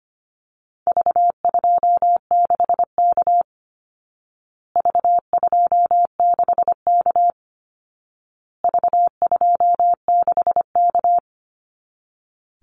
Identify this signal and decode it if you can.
Morse code, heard as a keyed tone, about 25 wpm, 700 Hz. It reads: V26K V26K V26K
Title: Call Signs Repeated 2x - 25wpm